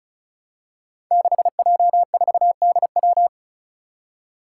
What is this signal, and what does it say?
Morse code, heard as a keyed tone, about 36 wpm, 700 Hz. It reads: 6J4DW
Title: Five Challenging Letter-Number Combinations 35wpm